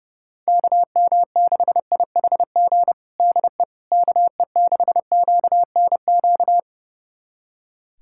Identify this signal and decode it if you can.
Morse code, heard as a keyed tone, about 32 wpm, 700 Hz. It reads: KM6IHG DE KE6QNQ
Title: USA Callsign DE Callsign - 6 Characters - 30wpm